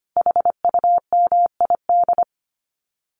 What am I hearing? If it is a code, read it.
Morse code, heard as a keyed tone, about 25 wpm, 700 Hz. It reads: HUMID
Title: Common QSO Phrases 25wpm